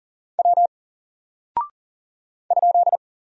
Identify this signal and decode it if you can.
Morse code, heard as a keyed tone, about 39 wpm, 700 Hz. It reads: W ?
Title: Single Character (Letter:Number:Punctuation:Prosign) 40wpm